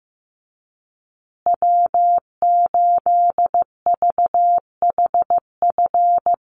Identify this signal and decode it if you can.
Morse code, heard as a keyed tone, about 15 wpm, 700 Hz. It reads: W8VHF
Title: USA Call Signs Repeated 2x 15wpm